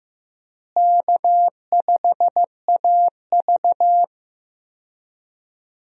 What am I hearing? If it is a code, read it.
Morse code, heard as a keyed tone, about 15 wpm, 700 Hz. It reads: K5AV
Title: USA Call Signs - 4 Characters - Repeated 3x - 15wpm